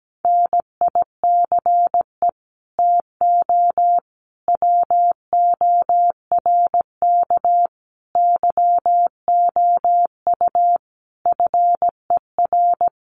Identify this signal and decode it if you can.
Morse code, heard as a keyed tone, about 17 wpm, 700 Hz. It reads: NICE TO WORK YOU FER
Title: Common QSO Phrases 17wpm